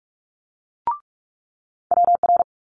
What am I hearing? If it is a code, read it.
Morse code, heard as a keyed tone, about 46 wpm, 700 Hz. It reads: WR
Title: Multi-letter Phonemes Quick Fire 45wpm